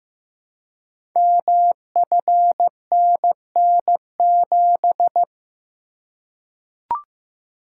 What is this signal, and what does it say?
Morse code, heard as a keyed tone, about 15 wpm, 700 Hz. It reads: MFNN7
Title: Code Groups - 5 Letter:Numbers - 15wpm